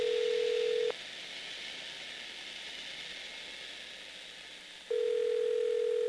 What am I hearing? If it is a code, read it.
Ringback tone (North America), 440 + 480 Hz, with noise.
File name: blackratchet / recordings / charles-st-payphone